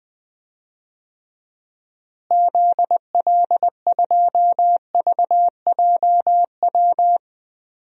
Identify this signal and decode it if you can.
Morse code, heard as a keyed tone, about 21 wpm, 700 Hz. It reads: ZL2VJW
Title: DXCC-100 Call Signs - 6 Characters - Repeated 3x - 20wpm